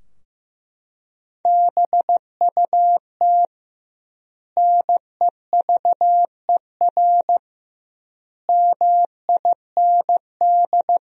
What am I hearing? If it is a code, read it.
Morse code, heard as a keyed tone, about 15 wpm, 700 Hz. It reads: BUT NEVER MIND
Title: 2x Word-Spacing - Sentences from Top 500 Words - 15wpm